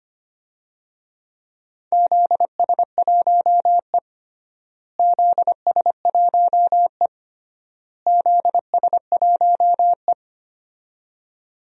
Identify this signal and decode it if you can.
Morse code, heard as a keyed tone, about 25 wpm, 700 Hz. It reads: ZS1E ZS1E ZS1E
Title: DXCC-100 Call Signs - 4 Characters - Repeated 3x - 25wpm